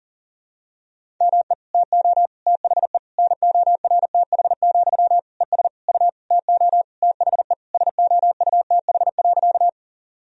Morse code, heard as a keyed tone, about 40 wpm, 700 Hz. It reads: ME TO THE NORTH, ES U TO THE SOUTH.
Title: Sentences from Top 400 Words and Abbreviations 40wpm